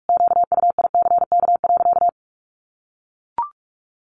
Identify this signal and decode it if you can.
Morse code, heard as a keyed tone, about 44 wpm, 700 Hz. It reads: QUICK.
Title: Sentences from Top 600 Words 45wpm